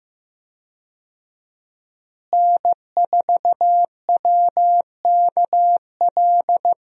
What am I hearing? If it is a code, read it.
Morse code, heard as a keyed tone, about 15 wpm, 700 Hz. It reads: N4WKL
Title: USA Call Signs - 5 Characters - Repeated 3x - 15wpm